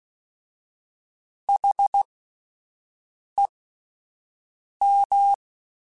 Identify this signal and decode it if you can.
Morse code, heard as a keyed tone, about 16 wpm, spaced as for 5 wpm, 800 Hz. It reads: HEM
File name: Misc / Misc-mp3s / WordPairs sp @10